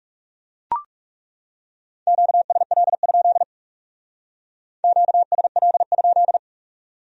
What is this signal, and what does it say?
Morse code, heard as a keyed tone, about 38 wpm, 700 Hz. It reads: QSL? QSL?